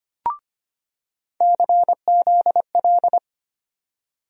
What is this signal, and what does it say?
Morse code, heard as a keyed tone, about 25 wpm, 700 Hz. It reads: CZL
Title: USA Call Sign Postfixes - 3 Letters 25wpm